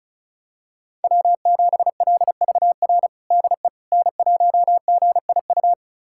Morse code, heard as a keyed tone, about 34 wpm, 690 Hz. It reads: W7LVR DE N1GIU
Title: USA Callsign DE Callsign - 5 Characters - 35wpm